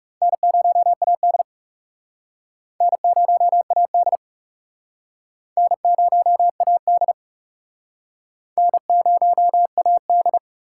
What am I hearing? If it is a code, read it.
Morse code, heard as a keyed tone, about 38 wpm, 700 Hz. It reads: N0AD N0AD N0AD N0AD
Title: Speed Racing - USA 1x2 Call Signs 30wpm